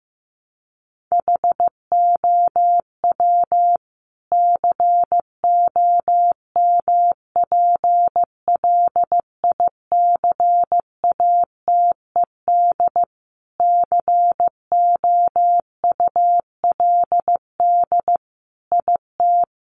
Morse code, heard as a keyed tone, about 15 wpm, 700 Hz. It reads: HOW COMPLICATED COULD IT
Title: Sentences from Top 1200 Words 15wpm